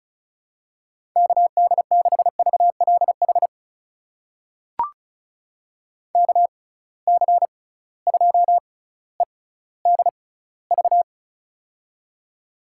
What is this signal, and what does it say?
Morse code, heard as a keyed tone, about 36 wpm, spaced as for 18 wpm, 700 Hz. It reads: KD6VLH KC2EDV
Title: USA Call Signs - 6 Characters - 6x Character Spacing - 35wpm